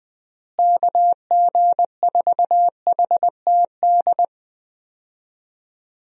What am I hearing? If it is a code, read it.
Morse code, heard as a keyed tone, about 20 wpm, 700 Hz. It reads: KG4HTD
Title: USA Call Signs - 6 Characters - Repeated 2x - 20wpm